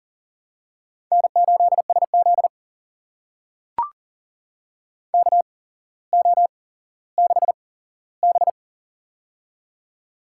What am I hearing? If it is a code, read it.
Morse code, heard as a keyed tone, about 40 wpm, spaced as for 17 wpm, 700 Hz. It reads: N8SZ KO6B
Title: USA Call Signs - 4 Characters - 8x Character Spacing - 40wpm